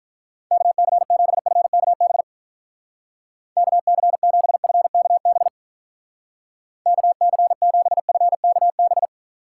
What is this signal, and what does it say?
Morse code, heard as a keyed tone, about 43 wpm, 700 Hz. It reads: KC7FKB KC7FKB KC7FKB
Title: Speed Racing - USA Call Signs 35wpm